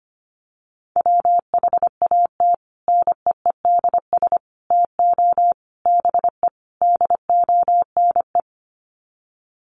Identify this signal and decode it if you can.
Morse code, heard as a keyed tone, about 25 wpm, 690 Hz. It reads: WHAT NEEDS TO BE DONE
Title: Sets of 5 Words Encore 25wpm